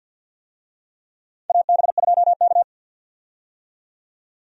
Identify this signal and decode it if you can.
Morse code, heard as a keyed tone, about 52 wpm, 700 Hz. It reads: AB2K
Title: USA Call Signs - 4 Characters - 50wpm